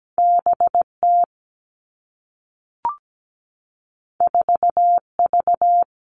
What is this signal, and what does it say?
Morse code, heard as a keyed tone, about 17 wpm, 700 Hz. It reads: B T 4 V
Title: Two Challenging Letter-Number Combinations 17wpm